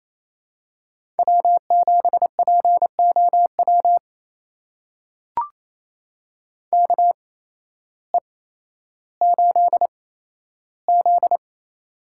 Morse code, heard as a keyed tone, about 28 wpm, spaced as for 13 wpm, 700 Hz. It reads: W7POW KE8Z
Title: USA Call Signs - 8x Character Spacing - 28wpm